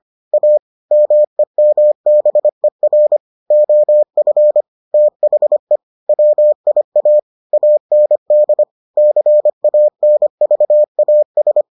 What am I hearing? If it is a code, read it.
Morse code, heard as a keyed tone, about 25 wpm, 600 Hz. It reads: A MEMBER OF THE WIA AND CANVAS